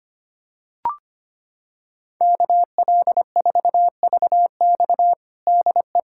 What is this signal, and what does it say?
Morse code, heard as a keyed tone, about 25 wpm, 700 Hz. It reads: KL4VX DE
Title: USA Callsign DE Callsign 25wpm